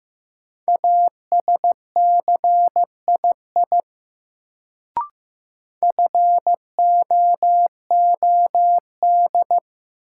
Morse code, heard as a keyed tone, about 15 wpm, 700 Hz. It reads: ASCII FOOD
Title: Sets of 5 Words - New Word Drill - 15wpm